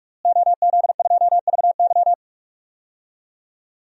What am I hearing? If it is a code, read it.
Morse code, heard as a keyed tone, about 46 wpm, 700 Hz. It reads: OZ2VY
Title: DXCC-100 Call Signs - 5 Characters - Repeated 3x - 45wpm